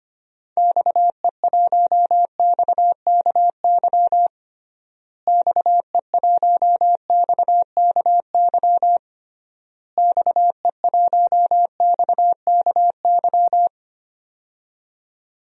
Morse code, heard as a keyed tone, about 25 wpm, 700 Hz. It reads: XE1XKY XE1XKY XE1XKY
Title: DXCC-100 Call Signs - 6 Characters - Repeated 3x - 25wpm